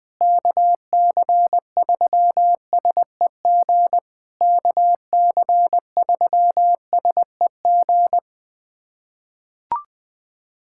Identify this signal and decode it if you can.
Morse code, heard as a keyed tone, about 20 wpm, 700 Hz. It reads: KC3SEG KC3SEG
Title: POTA QSO Practice - 20wpm